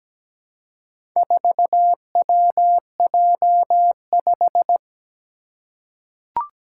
Morse code, heard as a keyed tone, about 17 wpm, 700 Hz. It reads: 4WJ5
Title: Four Challenging Letter-Number Combinations 17wpm